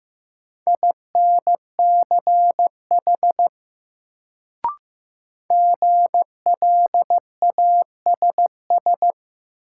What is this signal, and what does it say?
Morse code, heard as a keyed tone, about 15 wpm, 700 Hz. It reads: INCH GLASS
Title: Sentences from Top 1000 Words - Review Part 3 - New Word Drill - 15wpm